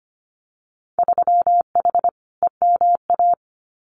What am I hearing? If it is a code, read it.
Morse code, heard as a keyed tone, about 25 wpm, 700 Hz. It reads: 3H EMA
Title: Winter Field Day Exchanges Encore 25wpm